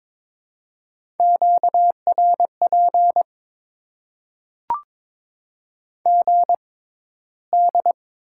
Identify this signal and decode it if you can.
Morse code, heard as a keyed tone, about 22 wpm, spaced as for 13 wpm, 700 Hz. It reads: QRP GD
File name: Top 90 QSO Elements - 6x Character Spacing - 22wpm